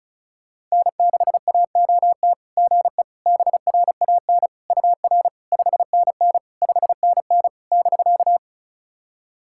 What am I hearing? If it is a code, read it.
Morse code, heard as a keyed tone, about 35 wpm, 700 Hz. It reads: N6AOT GE BRAN UR 5NN 5NN <BK>
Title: SOTA QSO Practice 35wpm